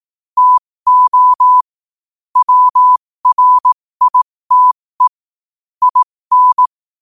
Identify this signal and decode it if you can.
Morse code, heard as a keyed tone, about 18 wpm, spaced as for 15 wpm, 1000 Hz. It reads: TO WRITE IN
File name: qotdfiles / 15wpm